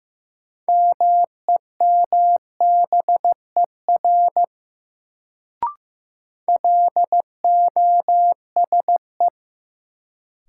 Morse code, heard as a keyed tone, about 15 wpm, 700 Hz. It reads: MEMBER LOSE
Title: Sentences from Top 1100 Words - New Word Drill - 15wpm